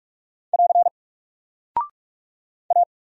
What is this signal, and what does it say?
Morse code, heard as a keyed tone, about 44 wpm, 700 Hz. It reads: <AR> A
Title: Single Character (Letter:Number:Punctuation:Prosign) 45wpm